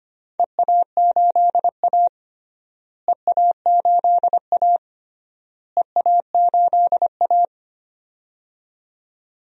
Morse code, heard as a keyed tone, about 25 wpm, 700 Hz. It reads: EA8A EA8A EA8A
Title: DXCC-100 Call Signs - 4 Characters - Repeated 3x - 25wpm